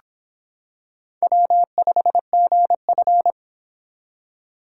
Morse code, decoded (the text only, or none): W5GF